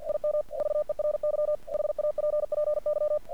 Morse code, heard as a keyed tone, about 47 wpm, 600 Hz. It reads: IM VERY HAPPY
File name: archived / CRSnet